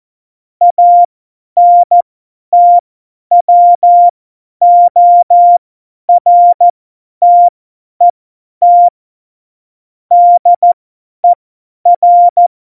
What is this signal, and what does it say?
Morse code, heard as a keyed tone, about 14 wpm, spaced as for 10 wpm, 700 Hz. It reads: ANTWORTET DER
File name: Morsekurs / Witze / Witz 09. Oktober 2018, 50 F70